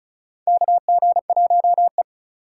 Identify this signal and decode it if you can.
Morse code, heard as a keyed tone, about 36 wpm, 700 Hz. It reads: KG1E
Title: USA Call Signs - 4 Characters - 3x Character Spacing - 35wpm